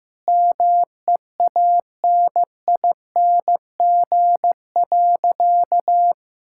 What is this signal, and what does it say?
Morse code, heard as a keyed tone, about 16 wpm, 700 Hz. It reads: MEANING.
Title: Sentences from Top 1000 Words - Review Part 6 - 15wpm